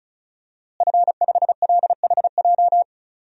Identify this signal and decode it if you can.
Morse code, heard as a keyed tone, about 34 wpm, 700 Hz. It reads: F5LHJ
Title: DXCC-100 Call Signs - 5 Characters - Repeated 3x - 35wpm